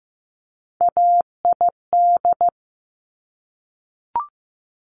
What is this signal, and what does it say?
Morse code, heard as a keyed tone, about 15 wpm, 700 Hz. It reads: AID
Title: Top Three Letter Combinations in Words 15wpm